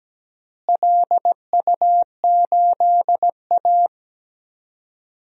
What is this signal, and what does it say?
Morse code, heard as a keyed tone, about 17 wpm, 700 Hz. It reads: LU8A
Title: DXCC-100 Call Signs - 4 Characters - Repeated 3x - 17wpm